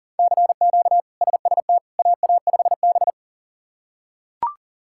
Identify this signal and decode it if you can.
Morse code, heard as a keyed tone, about 40 wpm, 700 Hz. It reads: CQ SST AA5B
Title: K1USN SST QSO Practice - 40wpm